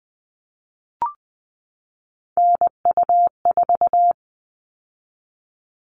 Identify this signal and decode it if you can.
Morse code, heard as a keyed tone, about 20 wpm, 700 Hz. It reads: NU4
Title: All Cut Numbers - 3-Digit 20wpm